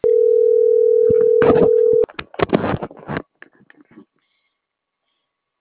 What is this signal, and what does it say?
Ringback tone (North America), 440 + 480 Hz.
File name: CENTRAL NAVIERAS / out juan.salcedo